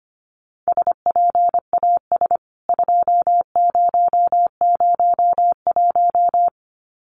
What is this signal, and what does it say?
Morse code, heard as a keyed tone, about 25 wpm, 700 Hz. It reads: SPAS 2001
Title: CWT Exchange 25wpm